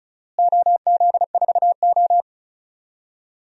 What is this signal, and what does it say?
Morse code, heard as a keyed tone, about 36 wpm, 700 Hz. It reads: OZ4O
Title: DXCC-100 Call Signs - 4 Characters - Repeated 2x - 35wpm